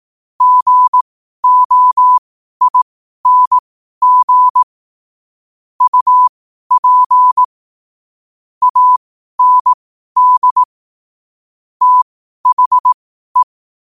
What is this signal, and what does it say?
Morse code, heard as a keyed tone, about 18 wpm, spaced as for 12 wpm, 1000 Hz. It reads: GOING UP AND THE
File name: qotdfiles / 12wpm